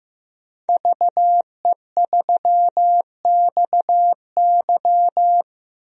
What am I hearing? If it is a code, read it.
Morse code, heard as a keyed tone, about 15 wpm, 700 Hz. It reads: VE3XY
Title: DXCC-100 Call Signs - 5 Characters - Repeated 3x - 15wpm